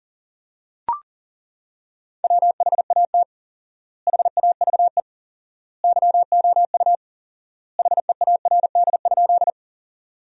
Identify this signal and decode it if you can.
Morse code, heard as a keyed tone, about 40 wpm, 700 Hz. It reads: WHAT HAVE YOU HEARD?
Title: 4x Word-Spacing - Sentences from Top 400 Words - 40wpm